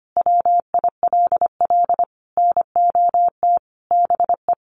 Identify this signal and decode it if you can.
Morse code, heard as a keyed tone, about 25 wpm, 700 Hz. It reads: WILL NOT BE
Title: Sentences from Top 800 Words 25wpm